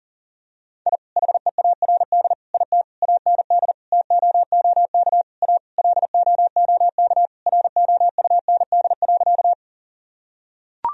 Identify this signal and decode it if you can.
Morse code, heard as a keyed tone, about 40 wpm, 700 Hz. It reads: I HEARD IT AND TOOK A LOOK ROUND.
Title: Sentences from Top 1000 Words - Review Part 1 - 40wpm